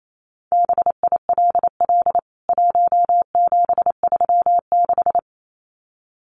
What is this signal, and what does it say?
Morse code, heard as a keyed tone, about 28 wpm, 700 Hz. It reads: BILL 1736
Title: CWT Exchange 28wpm